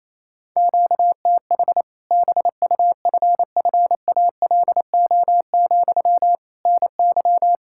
Morse code, heard as a keyed tone, about 28 wpm, 700 Hz. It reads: QTH BUFFALO, NY
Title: Practice QSO - Exchange 1 - Beginning - 28wpm